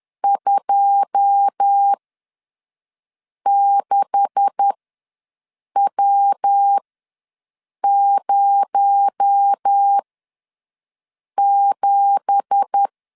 Morse code, heard as a keyed tone, about 10 wpm, spaced as for 5 wpm, 800 Hz. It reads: 26W07
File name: LEZIONI / 9 2867 ws 10wpm 4215